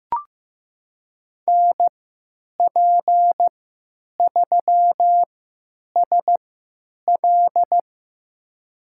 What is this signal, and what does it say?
Morse code, heard as a keyed tone, about 15 wpm, spaced as for 8 wpm, 700 Hz. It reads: NP3SL
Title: USA Call Signs - 5 Characters - 3x Character Spacing - 15wpm